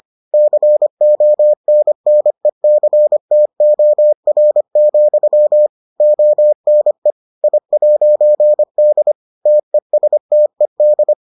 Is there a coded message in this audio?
Yes: Morse code from 0.3 to 11.1 s, heard as a keyed tone, about 25 wpm, 600 Hz. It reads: CONNECTOR, ONE I'D TESTED